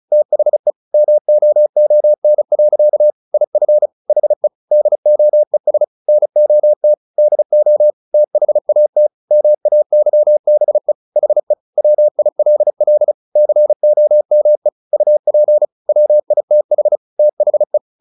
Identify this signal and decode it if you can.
Morse code, heard as a keyed tone, about 34 wpm, 600 Hz. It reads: THE MOON. IF HE DOES NOT DO THAT MAYBE HE WILL COME UP WITH THE
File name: the old man and the sea / the old man and the sea 1 35wpm